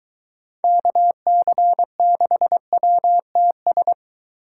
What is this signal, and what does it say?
Morse code, heard as a keyed tone, about 23 wpm, 700 Hz. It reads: KC6WTS